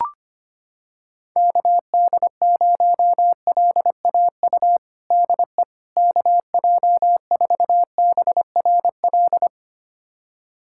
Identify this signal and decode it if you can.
Morse code, heard as a keyed tone, about 25 wpm, 700 Hz. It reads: KD0LAU DE KJ4BRL
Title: USA Callsign DE Callsign 25wpm